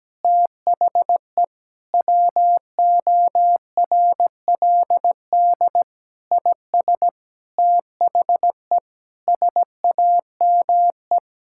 Morse code, heard as a keyed tone, about 17 wpm, 700 Hz. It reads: THE WORLD IS THE SAME
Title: Sentences from Top 1600 Words 17wpm